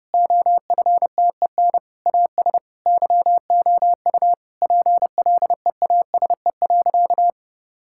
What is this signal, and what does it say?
Morse code, heard as a keyed tone, about 30 wpm, 700 Hz. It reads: OFTEN AS YOU PLEASE.
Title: Sentences from Top 1600 Words 30wpm